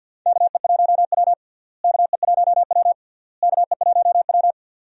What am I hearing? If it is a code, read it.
Morse code, heard as a keyed tone, about 48 wpm, 700 Hz. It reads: KE1W KE1W KE1W
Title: USA Call Signs - 4 Characters - Repeated 3x - 50wpm